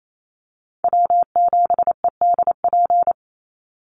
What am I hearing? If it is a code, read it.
Morse code, heard as a keyed tone, about 28 wpm, 700 Hz. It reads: W7EDP
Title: USA Call Signs 28wpm